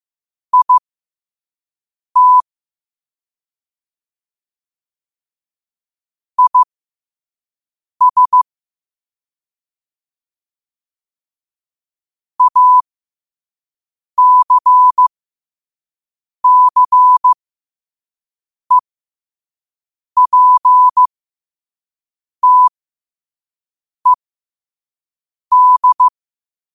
Morse code, heard as a keyed tone, about 15 wpm, spaced as for 5 wpm, 1000 Hz. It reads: IT IS ACCEPTED
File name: qotdfiles / 5wpm